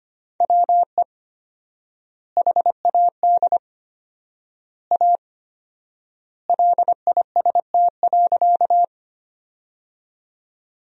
Morse code, heard as a keyed tone, about 25 wpm, 700 Hz. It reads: WE HAD A LIST.
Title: 4x Word-Spacing - Sentences from Top 400 Words - 25wpm